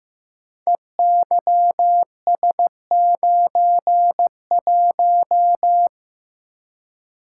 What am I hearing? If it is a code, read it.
Morse code, heard as a keyed tone, about 15 wpm, 700 Hz. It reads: EYS91
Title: Code Groups - 5 Letter:Numbers - 15wpm